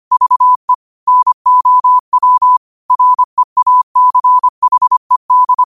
Morse code, heard as a keyed tone, about 25 wpm, 1000 Hz. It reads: VE NOW REACHED
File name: qotdfiles / 25wpm